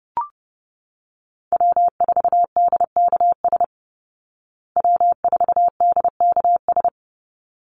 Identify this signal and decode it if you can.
Morse code, heard as a keyed tone, about 28 wpm, 700 Hz. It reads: W4DKS W4DKS
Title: Call Signs Repeated 3x - 30wpm